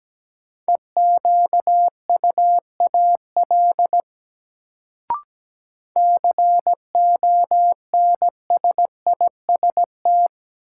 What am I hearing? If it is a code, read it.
Morse code, heard as a keyed tone, about 17 wpm, 700 Hz. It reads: EQUAL CONSIST
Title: Sentences from Top 1400 Words - New Word Drill Alt - 17wpm